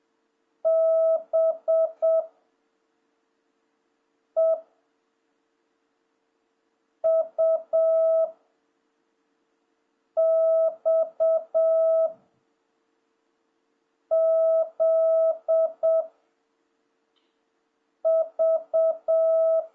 Morse code, heard as a keyed tone, about 7 wpm, 660 Hz. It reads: B E U X Z V
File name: jh8cbh / 2026cw